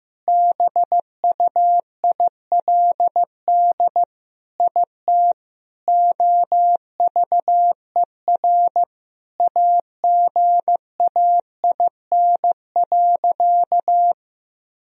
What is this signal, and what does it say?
Morse code, heard as a keyed tone, about 15 wpm, 700 Hz. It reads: BUILD IT OVER AGAIN.
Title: Sentences from Top 1000 Words - Review Part 1 - 15wpm